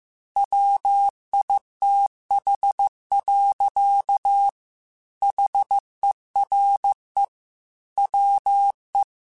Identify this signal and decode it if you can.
Morse code, heard as a keyed tone, about 15 wpm, 800 Hz. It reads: WITH. HERE WE
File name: Technical Subjects / Technical Subjects-mp3s / AND Gate @13